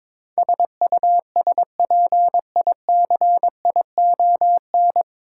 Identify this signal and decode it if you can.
Morse code, heard as a keyed tone, about 23 wpm, 700 Hz. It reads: SUSPICION
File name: 9 Letter Words - First 3 Letters Given - 22wpm